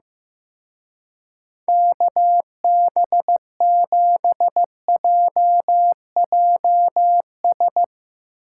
Morse code, heard as a keyed tone, about 16 wpm, 700 Hz. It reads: KB7JJS
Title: Speed Racing - USA 2x3 Call Signs 15wpm